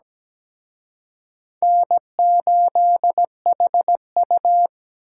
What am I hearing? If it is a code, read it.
Morse code, heard as a keyed tone, about 20 wpm, 700 Hz. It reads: N8HU
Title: Speed Racing - USA 1x2 Call Signs 17wpm